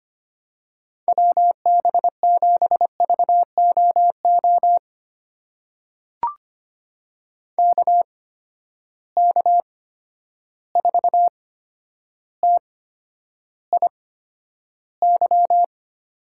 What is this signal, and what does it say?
Morse code, heard as a keyed tone, about 25 wpm, spaced as for 11 wpm, 700 Hz. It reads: WB7VOO KK4TIY
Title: USA Call Signs - 6 Characters - 8x Character Spacing - 25wpm